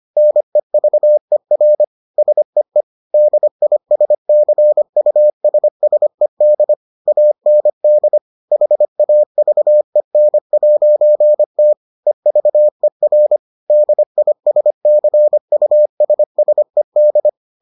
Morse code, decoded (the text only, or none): NEVER SEE DISCUSSED AND HAVEN'T EVER DISCUSSED